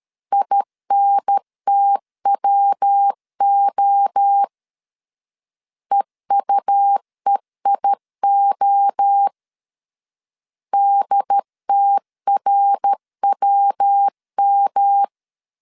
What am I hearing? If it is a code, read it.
Morse code, heard as a keyed tone, about 13 wpm, 800 Hz. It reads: INTWO EUEIO DTRWM
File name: CW / DET / 1 3 1500C S C 12wpm 2750